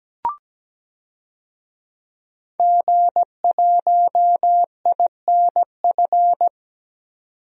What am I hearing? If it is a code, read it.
Morse code, heard as a keyed tone, about 17 wpm, 700 Hz. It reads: G1INF
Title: DXCC-100 Call Signs - 5 Characters - 17wpm